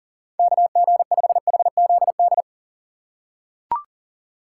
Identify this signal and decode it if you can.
Morse code, heard as a keyed tone, about 40 wpm, 700 Hz. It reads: KG5HZD
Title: USA Call Signs - 6 Characters - 4x Character Spacing - 40wpm